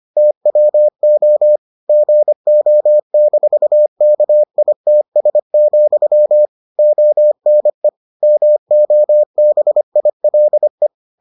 Morse code, heard as a keyed tone, about 25 wpm, 600 Hz. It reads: TWO GO-KITS, ONE MOBILE